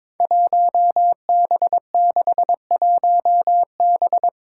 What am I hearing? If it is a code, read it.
Morse code, heard as a keyed tone, about 22 wpm, 700 Hz. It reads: 1B61B
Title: Five Challenging Letter-Number Combinations 22wpm